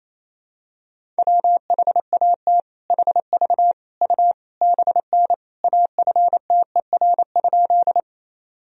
Morse code, heard as a keyed tone, about 28 wpm, 700 Hz. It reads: WHAT HV U BN AFTER?
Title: Sentences from Top 200 Words and Abbreviations 28wpm